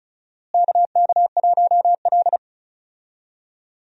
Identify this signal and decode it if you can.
Morse code, heard as a keyed tone, about 34 wpm, 700 Hz. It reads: KK1L
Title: USA Call Signs - 4 Characters - Repeated 2x - 35wpm